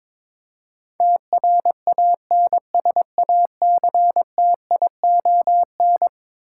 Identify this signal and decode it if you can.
Morse code, heard as a keyed tone, about 22 wpm, 700 Hz. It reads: TRANSACTION
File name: practice / 22wpm / Eleven Letter Words - 22wpm